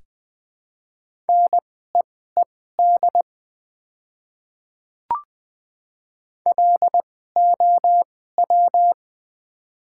Morse code, heard as a keyed tone, about 20 wpm, spaced as for 14 wpm, 700 Hz. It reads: NEED LOW
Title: 2x Character Spacing - Top 200 Words - 20wpm